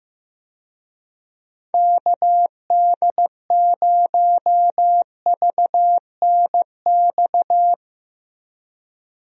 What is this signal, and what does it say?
Morse code, heard as a keyed tone, about 15 wpm, 700 Hz. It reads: KD0VNX
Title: USA Call Signs - 6 Characters - Repeated 2x - 15wpm